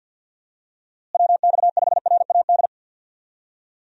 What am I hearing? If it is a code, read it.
Morse code, heard as a keyed tone, about 51 wpm, 700 Hz. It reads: WX5RAD
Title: USA Call Signs - 6 Characters - Repeated 3x - 50wpm